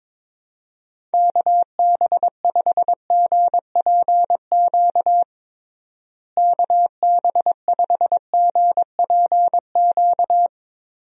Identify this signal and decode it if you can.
Morse code, heard as a keyed tone, about 22 wpm, 700 Hz. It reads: KB5GPQ KB5GPQ
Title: USA Call Signs - 6 Characters - Repeated 3x - 22wpm